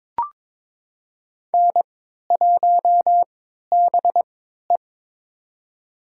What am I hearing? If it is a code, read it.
Morse code, heard as a keyed tone, about 22 wpm, spaced as for 12 wpm, 700 Hz. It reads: N1BE
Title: USA Call Signs - 3x Character Spacing - 22wpm